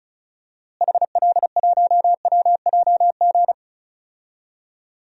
Morse code, heard as a keyed tone, about 34 wpm, 700 Hz. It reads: HL1WJG
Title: DXCC-100 Call Signs - 6 Characters - Repeated 2x - 35wpm